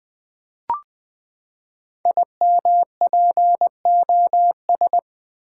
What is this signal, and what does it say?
Morse code, heard as a keyed tone, about 20 wpm, 700 Hz. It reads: IMPOS
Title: Build A Word - 5 Letters At A Time - 10-Letter Words - 20wpm